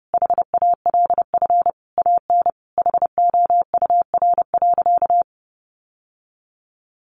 Morse code, heard as a keyed tone, about 30 wpm, 700 Hz. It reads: HALF AN HOUR.
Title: Sentences from Top 1100 Words 30wpm